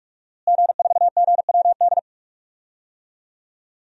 Morse code, heard as a keyed tone, about 46 wpm, 700 Hz. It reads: G4GWD
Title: DXCC-100 Call Signs - 5 Characters - Repeated 2x - 45wpm